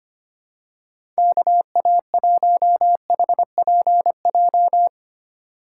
Morse code, heard as a keyed tone, about 25 wpm, 700 Hz. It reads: KA1HPJ
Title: USA Call Signs - 2x Character Spacing - 25wpm